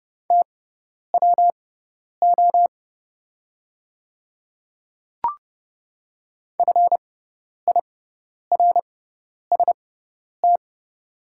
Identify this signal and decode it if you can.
Morse code, heard as a keyed tone, about 31 wpm, spaced as for 10 wpm, 700 Hz. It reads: TWO FIRST
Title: 6x Character Spacing - Top 100 Words - 30wpm